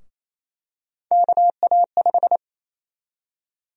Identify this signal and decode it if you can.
Morse code, heard as a keyed tone, about 28 wpm, 700 Hz. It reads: KA5
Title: USA Call Sign Prefixes - 28wpm